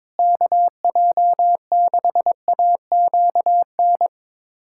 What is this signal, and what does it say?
Morse code, heard as a keyed tone, about 22 wpm, 700 Hz. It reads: KJ6AQN
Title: USA Callsign DE Callsign - 6 Characters - 22wpm